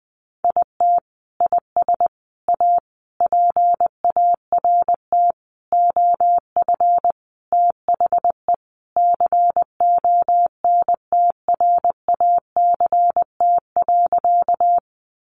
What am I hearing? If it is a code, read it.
Morse code, heard as a keyed tone, about 20 wpm, 700 Hz. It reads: IT IS A PART OF THE CONTRACT.
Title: Sentences from Top 1400 Words 20wpm